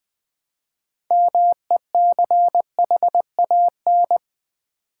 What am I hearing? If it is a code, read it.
Morse code, heard as a keyed tone, about 20 wpm, 700 Hz. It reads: MECHAN
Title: Build A Word - 2 Letters At A Time - 8-Letter Words - 20wpm